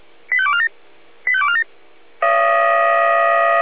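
Five-tone selective call (ZVEI), heard as quick runs of five tones, tones of about 80 ms. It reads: 85247 85247
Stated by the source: ZVEI five-tone selective call received on an SDR.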